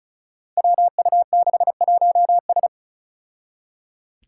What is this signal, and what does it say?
Morse code, heard as a keyed tone, about 36 wpm, 700 Hz. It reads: WU61S
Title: Five Challenging Letter-Number Combinations 35wpm